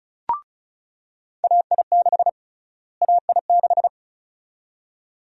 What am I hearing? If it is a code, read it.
Morse code, heard as a keyed tone, about 36 wpm, 700 Hz. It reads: AI6 AI6
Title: USA Call Sign Prefixes - Repeated 2x - 35wpm